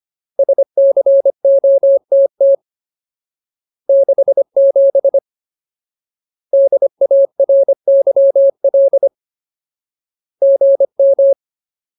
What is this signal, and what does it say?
Morse code, heard as a keyed tone, about 25 wpm, 550 Hz. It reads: SCOTT 67 DARYL GM
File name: Ultimate Word List / 25WPM Ultimate